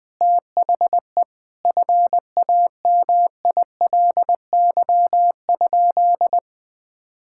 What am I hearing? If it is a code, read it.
Morse code, heard as a keyed tone, about 21 wpm, 700 Hz. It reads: THE FAMILY?
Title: Sentences from Top 1000 Words - Review Part 6 - 20wpm